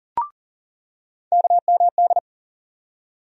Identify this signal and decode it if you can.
Morse code, heard as a keyed tone, about 40 wpm, 700 Hz. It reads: KMD
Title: USA Call Sign Postfixes - 3 Letters 40wpm